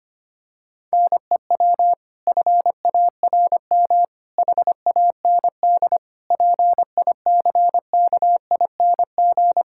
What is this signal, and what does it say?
Morse code, heard as a keyed tone, about 25 wpm, 700 Hz. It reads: NEW FARM HAND PICKING
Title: Pangram Challenge v2 - 25wpm